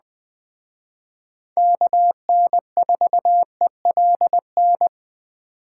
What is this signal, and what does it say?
Morse code, heard as a keyed tone, about 20 wpm, 700 Hz. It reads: KN4ELN
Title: Speed Racing - USA 2x3 Call Signs 17wpm